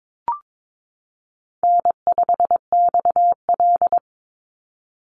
Morse code, heard as a keyed tone, about 22 wpm, 700 Hz. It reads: N5XL
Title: NA SOTA Chaser Callsigns 22wpm